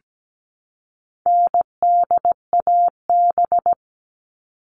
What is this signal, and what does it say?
Morse code, heard as a keyed tone, about 17 wpm, 700 Hz. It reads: NDAB
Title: All Cut Numbers - 4-Digit 17wpm